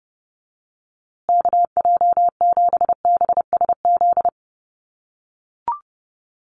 Morse code, heard as a keyed tone, about 30 wpm, 700 Hz. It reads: KJ7BSZ
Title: USA Call Signs Repeated 2x 30wpm